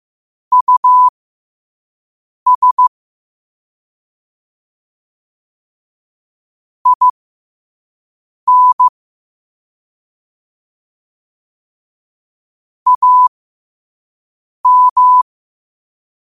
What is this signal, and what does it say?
Morse code, heard as a keyed tone, about 15 wpm, spaced as for 5 wpm, 1000 Hz. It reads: US IN AM
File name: qotdfiles / 5wpm